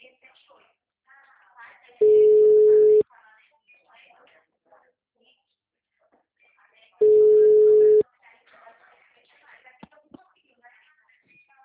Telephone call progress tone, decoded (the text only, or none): ringback tone